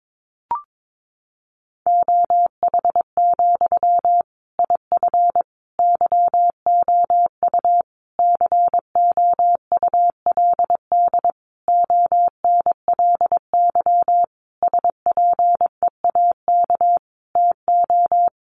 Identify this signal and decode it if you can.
Morse code, heard as a keyed tone, about 22 wpm, 700 Hz. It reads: OH, IF YOU COULD ONLY SPEAK TO
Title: Sentences from Top 1100 Words 22wpm